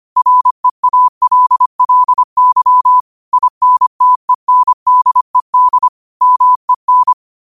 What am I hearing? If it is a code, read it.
Morse code, heard as a keyed tone, about 25 wpm, 1000 Hz. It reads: REALLY INTENDED MEN
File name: qotdfiles / 25wpm